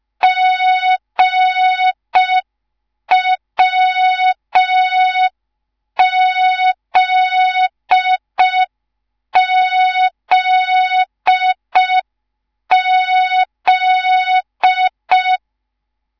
Morse code, heard as a keyed tone, about 5 wpm, 3700 Hz. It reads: GWZZZ